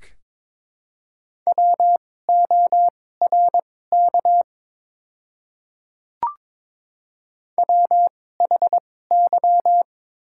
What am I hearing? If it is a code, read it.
Morse code, heard as a keyed tone, about 22 wpm, spaced as for 16 wpm, 690 Hz. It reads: WORK WHY
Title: 2x Character Spacing - Top 500 Words - 22wpm